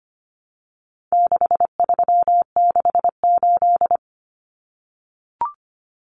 Morse code, heard as a keyed tone, about 25 wpm, 700 Hz. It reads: 6368
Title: Common Cut Numbers - 4-Digit 25wpm